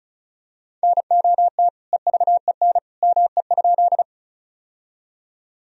Morse code, heard as a keyed tone, about 36 wpm, 700 Hz. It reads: NOT EVEN ME?
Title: Sentences from Top 1000 Words - Review Part 1 - 35wpm